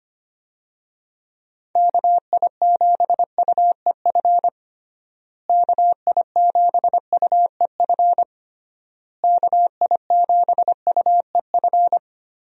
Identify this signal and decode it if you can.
Morse code, heard as a keyed tone, about 25 wpm, 700 Hz. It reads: KI7UEF KI7UEF KI7UEF
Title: USA Call Signs - 6 Characters - Repeated 3x - 25wpm